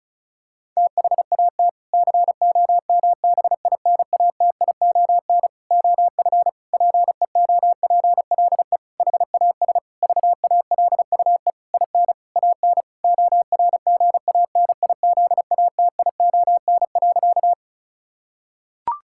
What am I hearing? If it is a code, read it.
Morse code, heard as a keyed tone, about 36 wpm, 700 Hz. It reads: THAT COMBINATION OF PEOPLE HAS VALUE IN AN ORGANIZATION.
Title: Sentences from Top 1000 Words - Review Part 6 - 35wpm